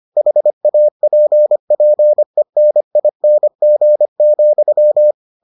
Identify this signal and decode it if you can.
Morse code, heard as a keyed tone, about 25 wpm, 600 Hz. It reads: HAPPENING,